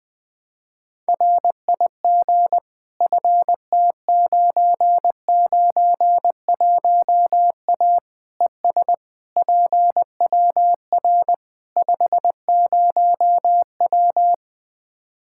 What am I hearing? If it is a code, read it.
Morse code, heard as a keyed tone, about 21 wpm, 700 Hz. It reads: RIG FT991A ES PWR 50W
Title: Practice QSO - Exchange 2 - Info - 20wpm